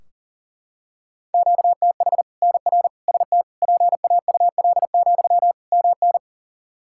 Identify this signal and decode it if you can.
Morse code, heard as a keyed tone, about 42 wpm, 700 Hz. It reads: QTH NR ST PAUL, MN
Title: K1USN SST QSO Practice 40wpm